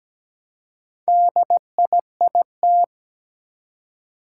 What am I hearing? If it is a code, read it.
Morse code, heard as a keyed tone, about 17 wpm, 700 Hz. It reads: DIIT
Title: Four Challenging Letter-Number Combinations 17wpm